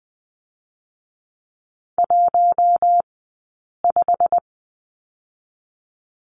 Morse code, heard as a keyed tone, about 20 wpm, 700 Hz. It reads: 1 5
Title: 086 Lesson 15d - Number 5 - Two Characters 20wpm